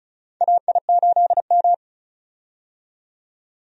Morse code, heard as a keyed tone, about 36 wpm, 700 Hz. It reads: AI8M
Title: USA Call Signs - 4 Characters - 35wpm